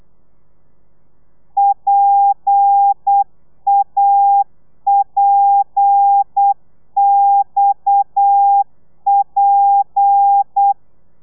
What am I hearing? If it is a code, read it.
Morse code, heard as a keyed tone, about 8 wpm, 800 Hz. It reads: PAPXP